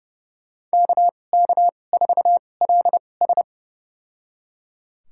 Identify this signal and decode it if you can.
Morse code, heard as a keyed tone, about 31 wpm, spaced as for 22 wpm, 700 Hz. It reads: KK4LS
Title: USA Call Signs - 5 Characters - 2x Character Spacing - 30wpm